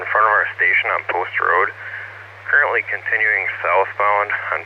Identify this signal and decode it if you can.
Emergency Alert System (SAME) tones, heard as harsh buzzing bursts, with speech. It is heard from 0.4 to 3.6 s.